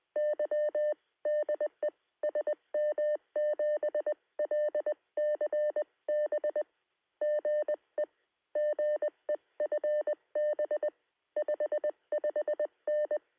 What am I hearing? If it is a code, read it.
Morse code, heard as a keyed tone, about 20 wpm, 600 Hz. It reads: Y DE SM7LCB GE GE FB 55N
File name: sm7lcb / 2115z g3xdy 23cm